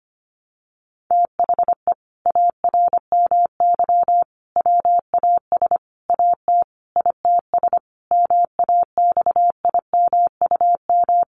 Morse code, heard as a keyed tone, about 25 wpm, 700 Hz. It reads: THE ARMY WAS AT ITS MAXIMUM
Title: Sentences from Top 1300 Words 25wpm